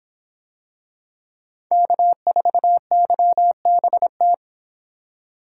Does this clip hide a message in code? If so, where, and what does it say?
Yes: Morse code from 1.7 to 4.3 s, heard as a keyed tone, about 25 wpm, 700 Hz. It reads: K4YBT